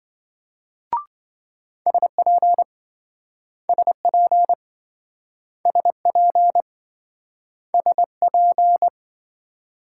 Morse code, heard as a keyed tone, about 27 wpm, 700 Hz. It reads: SP SP SP SP